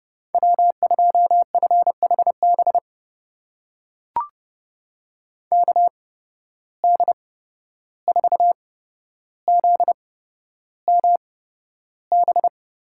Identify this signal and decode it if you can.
Morse code, heard as a keyed tone, about 30 wpm, spaced as for 12 wpm, 700 Hz. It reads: W2FHB KD4ZMB
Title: USA Call Signs - 8x Character Spacing - 30wpm